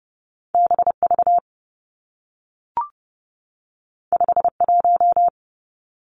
Morse code, heard as a keyed tone, about 32 wpm, 700 Hz. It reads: B V 5 1
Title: Two Challenging Letter-Number Combinations 30wpm